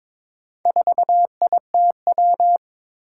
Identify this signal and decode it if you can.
Morse code, heard as a keyed tone, about 22 wpm, 700 Hz. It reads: 4ITW
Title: Four Challenging Letter-Number Combinations 22wpm